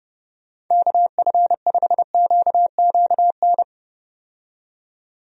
Morse code, heard as a keyed tone, about 30 wpm, 700 Hz. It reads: KF5QQN
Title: USA Call Signs - 6 Characters - 30wpm